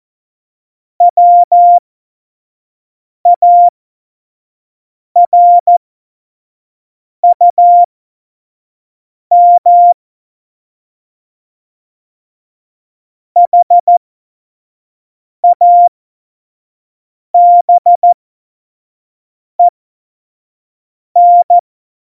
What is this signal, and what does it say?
Morse code, heard as a keyed tone, about 14 wpm, spaced as for 5 wpm, 700 Hz. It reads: WARUM HABEN
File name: Morsekurs / Witze / Witz 09. Oktober 2018, 25 F70